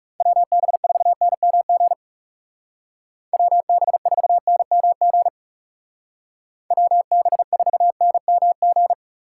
Morse code, heard as a keyed tone, about 40 wpm, 700 Hz. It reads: WB4NMG WB4NMG WB4NMG